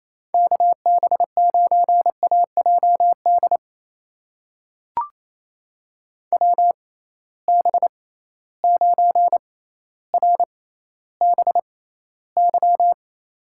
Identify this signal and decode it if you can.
Morse code, heard as a keyed tone, about 28 wpm, spaced as for 14 wpm, 700 Hz. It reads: KB9AJD WB9RBY
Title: USA Call Signs - 6 Characters - 6x Character Spacing - 28wpm